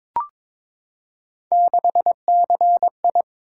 Morse code, heard as a keyed tone, about 22 wpm, 700 Hz. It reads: 6CI
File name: practice / 22wpm / Three Letter-Number Combinations - 22wpm